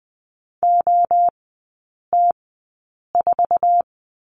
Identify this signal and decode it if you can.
Morse code, heard as a keyed tone, about 20 wpm, spaced as for 8 wpm, 700 Hz. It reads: OT4
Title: 053 Lesson 10e - Letter r - Three Characters 20wpm